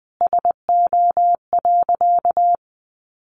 Morse code, heard as a keyed tone, about 20 wpm, 700 Hz. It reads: SO.
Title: Sentences from Top 1400 Words 20wpm